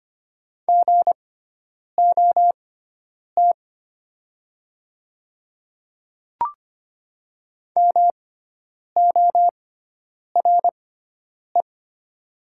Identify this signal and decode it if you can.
Morse code, heard as a keyed tone, about 25 wpm, spaced as for 9 wpm, 700 Hz. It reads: GOT MORE
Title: 6x Character Spacing - Top 300 Words - 25wpm